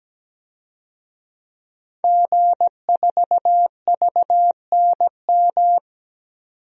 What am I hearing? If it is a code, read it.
Morse code, heard as a keyed tone, about 17 wpm, 700 Hz. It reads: G4VNM
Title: DXCC-100 Call Signs - 5 Characters - 17wpm